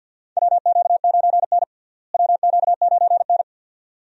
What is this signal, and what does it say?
Morse code, heard as a keyed tone, about 52 wpm, 700 Hz. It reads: WQ9N WQ9N
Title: USA Call Signs - 4 Characters - Repeated 2x - 50wpm